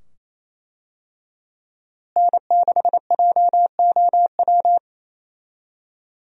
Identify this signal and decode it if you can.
Morse code, heard as a keyed tone, about 28 wpm, 700 Hz. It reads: N6JOW
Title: USA Call Signs - 5 Characters - Repeated 2x - 28wpm